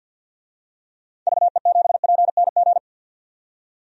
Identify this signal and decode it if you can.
Morse code, heard as a keyed tone, about 50 wpm, 700 Hz. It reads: VE7PNG
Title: DXCC-100 Call Signs - 6 Characters - 50wpm